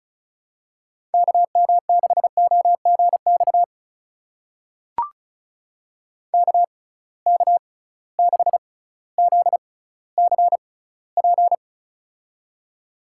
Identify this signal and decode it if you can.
Morse code, heard as a keyed tone, about 34 wpm, spaced as for 18 wpm, 700 Hz. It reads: KM6OGX KK6ZCP
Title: USA Call Signs - 6 Characters - 6x Character Spacing - 35wpm